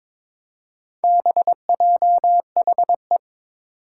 Morse code, heard as a keyed tone, about 22 wpm, 700 Hz. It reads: BJHE
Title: Four Challenging Letter-Number Combinations 22wpm